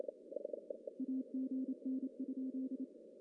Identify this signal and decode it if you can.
Morse code, heard as a keyed tone, about 28 wpm, 260 Hz. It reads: AGN?
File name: CW / cqww-cw-k5tr